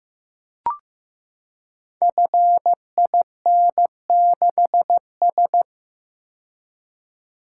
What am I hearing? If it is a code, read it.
Morse code, heard as a keyed tone, about 15 wpm, 700 Hz. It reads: FIN6S
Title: Code Groups - 5 Letter:Numbers - 15wpm